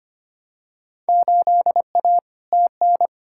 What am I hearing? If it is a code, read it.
Morse code, heard as a keyed tone, about 25 wpm, 700 Hz. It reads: 8A TN
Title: Field Day Exchanges Encore - 25wpm